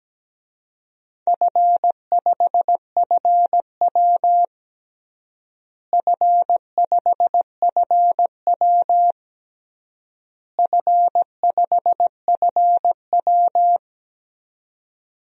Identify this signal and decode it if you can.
Morse code, heard as a keyed tone, about 17 wpm, 700 Hz. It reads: F5FW F5FW F5FW
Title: DXCC-100 Call Signs - 4 Characters - Repeated 3x - 17wpm